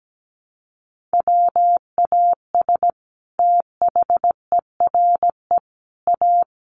Morse code, heard as a keyed tone, about 17 wpm, 700 Hz. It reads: WAS THERE A
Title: Sentences from Top 1300 Words 17wpm